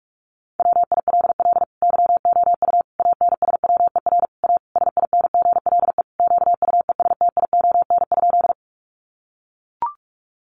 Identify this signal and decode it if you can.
Morse code, heard as a keyed tone, about 44 wpm, 700 Hz. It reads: WILL YOU ANSWER A SINGLE QUESTION?
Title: Sentences from Top 700 Words 45wpm